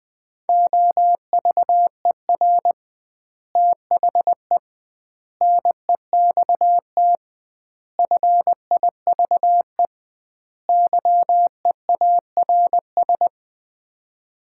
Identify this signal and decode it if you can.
Morse code, heard as a keyed tone, about 21 wpm, 700 Hz. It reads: OVER THE NEXT FIVE YEARS
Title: 2x Word-Spacing - Sets of 5 Words - 20wpm